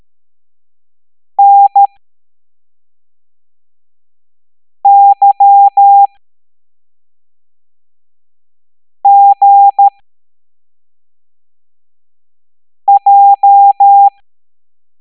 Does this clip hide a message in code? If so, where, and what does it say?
Yes: Morse code from 1.4 to 14.1 s, heard as a keyed tone, about 13 wpm, spaced as for 3 wpm, 800 Hz. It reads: NYGJ